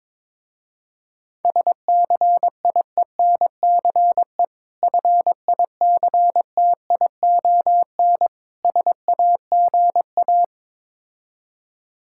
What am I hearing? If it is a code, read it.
Morse code, heard as a keyed tone, about 22 wpm, 690 Hz. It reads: SCIENCE FICTION SAGA
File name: Difficult Words with Hints - 22wpm